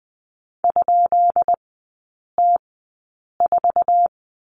Morse code, heard as a keyed tone, about 20 wpm, spaced as for 8 wpm, 700 Hz. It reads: ?T4
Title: 133 Lesson 22e - Question Mark - Three Characters 20wpm